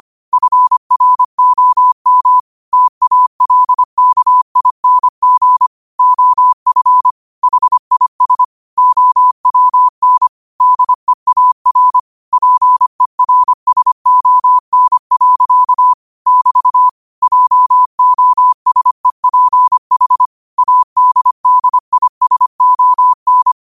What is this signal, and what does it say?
Morse code, heard as a keyed tone, about 25 wpm, 1000 Hz. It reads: FROM TALKING OF HIS OWN DEAR PERSON. <BT> JOSEPH ADDISON